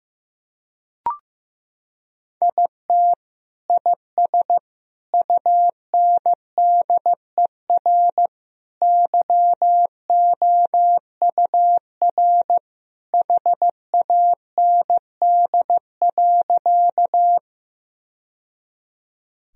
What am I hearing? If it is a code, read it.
Morse code, heard as a keyed tone, about 15 wpm, 700 Hz. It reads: IT IS UNDER YOUR HAND.
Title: Sentences from Top 1000 Words - Review Part 1 - 15wpm